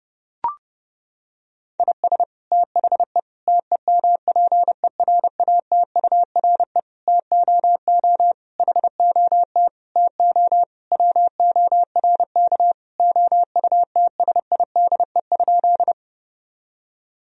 Morse code, heard as a keyed tone, about 30 wpm, 700 Hz. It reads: IS THE TEMPERATURE TOO HOT TO WORK OUTSIDE?
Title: Sentences from Top 1000 Words - Review Part 4 - 30wpm